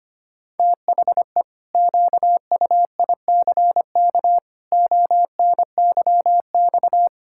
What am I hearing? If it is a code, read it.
Morse code, heard as a keyed tone, about 25 wpm, 700 Hz. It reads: THE QUICK ONYX
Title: Pangram Challenge v2 - 25wpm